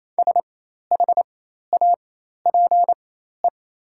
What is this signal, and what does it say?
Morse code, heard as a keyed tone, about 28 wpm, spaced as for 13 wpm, 700 Hz. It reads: SHAPE
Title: 4x Character Spacing - Top 500 Words - 28wpm